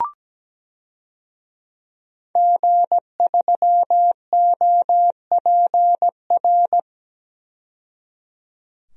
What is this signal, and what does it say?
Morse code, heard as a keyed tone, about 17 wpm, 700 Hz. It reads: G3OPR
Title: DXCC-100 Call Signs - 5 Characters - 17wpm